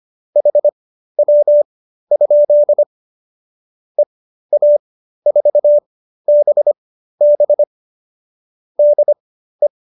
Morse code, heard as a keyed tone, about 25 wpm, spaced as for 13 wpm, 590 Hz. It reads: HW? EA4BB DE
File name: qso215b 13